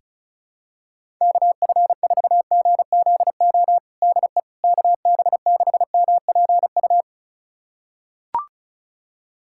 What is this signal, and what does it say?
Morse code, heard as a keyed tone, about 34 wpm, 700 Hz. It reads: KF4GZO DE KB6MPU
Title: USA Callsign DE Callsign - 6 Characters - 35wpm